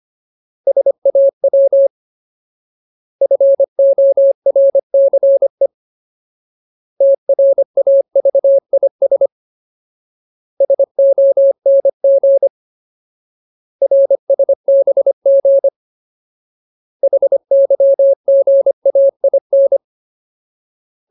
Morse code, heard as a keyed tone, about 25 wpm, 550 Hz. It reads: SAW FORCE TRAVIS SONG RSBG HYGAIN
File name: Ultimate Word List / 25WPM Ultimate